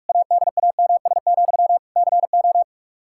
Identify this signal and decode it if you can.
Morse code, heard as a keyed tone, about 46 wpm, 700 Hz. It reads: ADAMS, CO
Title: Practice QSO - Exchange 1 - Beginning - 45wpm